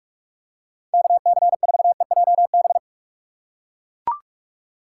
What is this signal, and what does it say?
Morse code, heard as a keyed tone, about 44 wpm, 700 Hz. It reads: KC4EJB
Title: USA Call Signs - 6 Characters - 6x Character Spacing - 45wpm